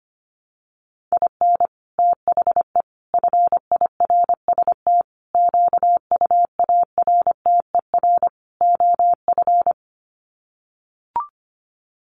Morse code, heard as a keyed tone, about 25 wpm, 700 Hz. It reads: IN THE FIRST QUARTER OF
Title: Sets of 5 Words Encore2 25wpm